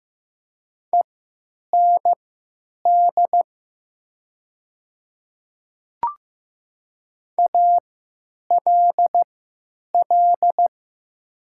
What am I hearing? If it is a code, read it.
Morse code, heard as a keyed tone, about 15 wpm, spaced as for 9 wpm, 700 Hz. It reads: END ALL
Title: 3x Character Spacing - Top 200 Words - 15wpm